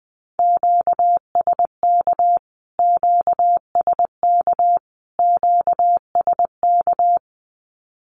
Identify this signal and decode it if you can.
Morse code, heard as a keyed tone, about 20 wpm, 700 Hz. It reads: QSK QSK QSK
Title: Q-Codes Repeated 3x 20wpm